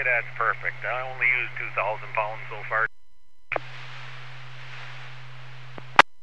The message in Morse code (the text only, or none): none